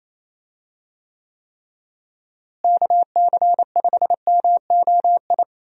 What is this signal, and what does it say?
Morse code, heard as a keyed tone, about 28 wpm, 700 Hz. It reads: KC5MOI
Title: USA Call Signs - 6 Characters - Repeated 3x - 28wpm